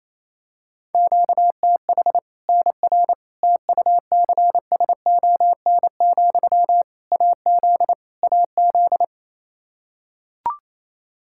Morse code, heard as a keyed tone, about 28 wpm, 700 Hz. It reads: QTH NR TUCSON, AZ AZ
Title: Practice QSO - Exchange 1 - Beginning - 28wpm